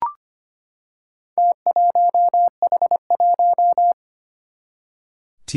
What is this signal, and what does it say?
Morse code, heard as a keyed tone, about 25 wpm, 700 Hz. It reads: T1H1
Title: Four Challenging Letter-Number Combinations 25wpm